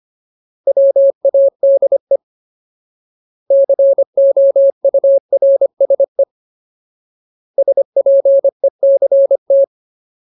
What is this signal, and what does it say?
Morse code, heard as a keyed tone, about 25 wpm, 550 Hz. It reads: WADE COURSE SPECT
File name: Ultimate Word List / 25WPM Ultimate